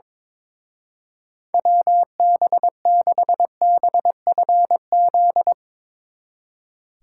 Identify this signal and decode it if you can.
Morse code, heard as a keyed tone, about 24 wpm, 700 Hz. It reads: WB6BFZ
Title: Speed Racing - USA 2x3 Call Signs 22wpm